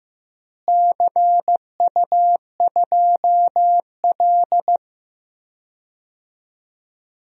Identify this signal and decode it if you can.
Morse code, heard as a keyed tone, about 15 wpm, 700 Hz. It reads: CU2L
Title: DXCC-100 Call Signs - 4 Characters - Repeated 3x - 15wpm